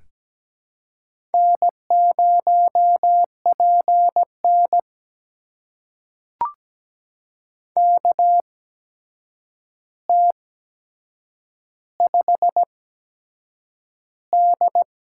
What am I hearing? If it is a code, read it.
Morse code, heard as a keyed tone, about 17 wpm, spaced as for 7 wpm, 700 Hz. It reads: N0PN KT5D
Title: USA Call Signs - 4 Characters - 8x Character Spacing - 17wpm